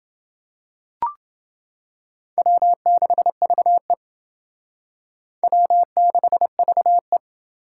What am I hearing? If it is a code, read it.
Morse code, heard as a keyed tone, about 29 wpm, 700 Hz. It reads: W6VE W6VE